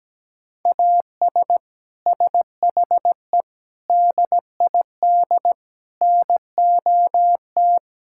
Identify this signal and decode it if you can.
Morse code, heard as a keyed tone, about 17 wpm, 700 Hz. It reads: AS SHE DID NOT
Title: Sentences from Top 1000 Words - Review Part 5 - 17wpm